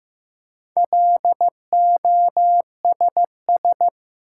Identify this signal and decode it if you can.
Morse code, heard as a keyed tone, about 15 wpm, 700 Hz. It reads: LOSS
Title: Four Letter Words - 15wpm